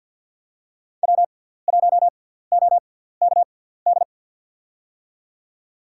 Morse code, heard as a keyed tone, about 51 wpm, spaced as for 17 wpm, 700 Hz. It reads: W1OKD
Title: USA Call Signs - 5 Characters - 6x Character Spacing - 50wpm